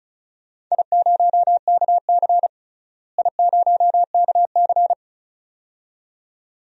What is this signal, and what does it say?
Morse code, heard as a keyed tone, about 36 wpm, 700 Hz. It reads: I0KC I0KC
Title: DXCC-100 Call Signs - 4 Characters - Repeated 2x - 35wpm